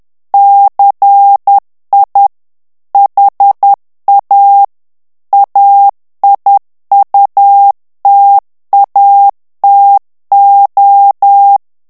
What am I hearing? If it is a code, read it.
Morse code, heard as a keyed tone, about 10 wpm, 800 Hz. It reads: CI HA AIUTATO
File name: morse mp3 / ITALIEN texte clair 1 vitesse 10 mots minute